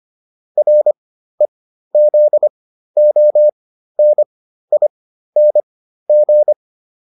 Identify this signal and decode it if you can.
Morse code, heard as a keyed tone, about 25 wpm, spaced as for 13 wpm, 600 Hz. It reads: REZONING